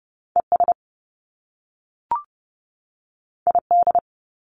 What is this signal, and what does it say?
Morse code, heard as a keyed tone, about 30 wpm, 700 Hz. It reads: ES ID
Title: Top Two Letter Combinations in Words 30wpm